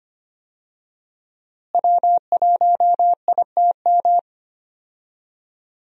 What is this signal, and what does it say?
Morse code, heard as a keyed tone, about 25 wpm, 700 Hz. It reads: W1ITM
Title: USA Call Signs - 5 Characters - Repeated 2x - 25wpm